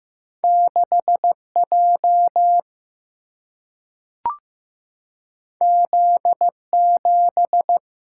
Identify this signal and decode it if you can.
Morse code, heard as a keyed tone, about 15 wpm, 700 Hz. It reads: 6 J Z 7
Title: Single-Letter Single-Number 15wpm